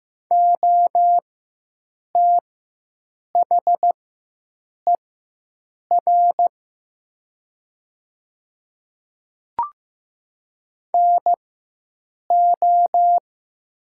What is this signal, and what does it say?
Morse code, heard as a keyed tone, about 15 wpm, spaced as for 7 wpm, 700 Hz. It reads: OTHER NO
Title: 4x Character Spacing - Top 100 Words - 15wpm